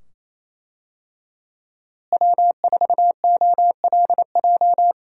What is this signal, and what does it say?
Morse code, heard as a keyed tone, about 28 wpm, 700 Hz. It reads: W4OLJ
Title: USA Call Signs - 5 Characters - Repeated 2x - 28wpm